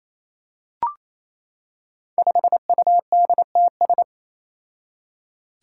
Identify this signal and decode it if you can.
Morse code, heard as a keyed tone, about 27 wpm, 700 Hz. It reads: 5UDTS
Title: Five Challenging Letter-Number Combinations 28wpm